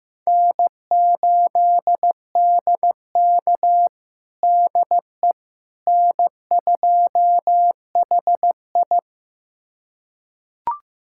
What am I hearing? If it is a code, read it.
Morse code, heard as a keyed tone, about 15 wpm, 700 Hz. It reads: N8DK DE N2HI
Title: USA Callsign DE Callsign - 4 Characters - 15wpm